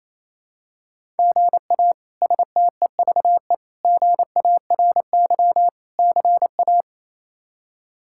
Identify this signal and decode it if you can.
Morse code, heard as a keyed tone, about 28 wpm, 700 Hz. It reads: GA STEVE GARY CA
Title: K1USN SST QSO Practice - 28wpm